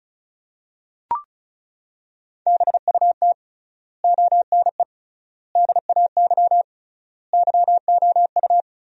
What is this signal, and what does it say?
Morse code, heard as a keyed tone, about 34 wpm, 700 Hz. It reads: BUT ONE DAY YOU
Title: 3x Word-Spacing - Sentences from Top 400 Words - 35wpm